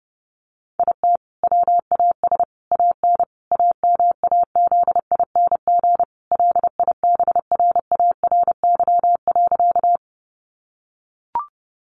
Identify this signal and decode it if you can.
Morse code, heard as a keyed tone, about 30 wpm, 700 Hz. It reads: IT WAS AN AMAZING LIBRARY.
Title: Sentences from Top 1500 Words 30wpm